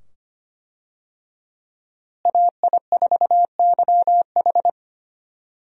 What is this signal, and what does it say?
Morse code, heard as a keyed tone, about 25 wpm, 700 Hz. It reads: AI4YH
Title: USA Call Signs - 5 Characters - Repeated 2x - 25wpm